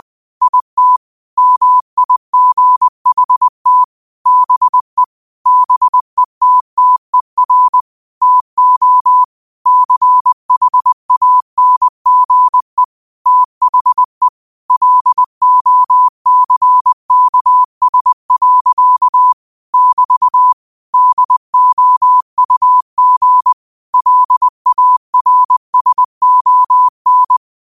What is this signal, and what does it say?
Morse code, heard as a keyed tone, about 20 wpm, 1000 Hz. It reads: IT MIGHT BE BETTER TO CHANGE THE LOCKS. <BT> DOUG LARSON
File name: qotdfiles / 20wpm